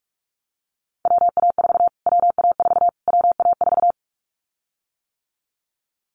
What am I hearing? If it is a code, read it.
Morse code, heard as a keyed tone, about 46 wpm, 700 Hz. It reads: WA4 WA4 WA4
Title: Callsign Prefixes for Top 25 DXCC - Repeated 3x - 45wpm